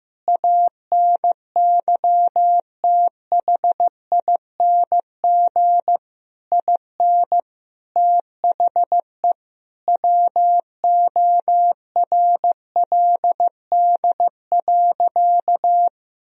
Morse code, heard as a keyed tone, about 15 wpm, 700 Hz. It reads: ANYTHING IN THE WORLD.
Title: Sentences from Top 1000 Words - Review Part 3 - 15wpm